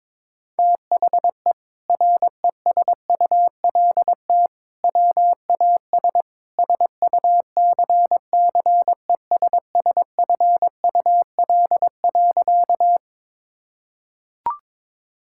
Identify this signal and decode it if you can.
Morse code, heard as a keyed tone, about 22 wpm, 700 Hz. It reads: THE RESULT WAS SUCCESSFUL.
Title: Sentences from Top 1000 Words - Review Part 6 - 22wpm